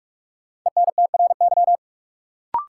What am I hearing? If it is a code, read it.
Morse code, heard as a keyed tone, about 45 wpm, 700 Hz. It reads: ENTRY
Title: Sentences from Top 1500 Words - New Word Drill Alt - 45wpm